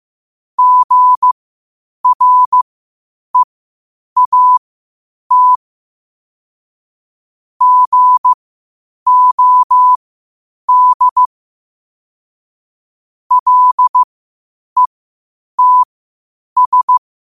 Morse code, heard as a keyed tone, about 15 wpm, spaced as for 8 wpm, 1000 Hz. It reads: GREAT GOD LETS
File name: 8wpm